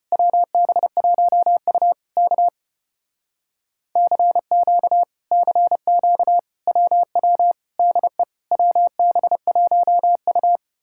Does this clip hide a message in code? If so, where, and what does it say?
Yes: Morse code from 0.1 to 10.6 s, heard as a keyed tone, about 33 wpm, 700 Hz. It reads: WB1U K CQ CQ WW DE WB1U